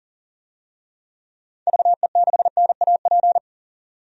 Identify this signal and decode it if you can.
Morse code, heard as a keyed tone, about 40 wpm, 700 Hz. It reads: VE6NAP
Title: DXCC-100 Call Signs - 6 Characters - 40wpm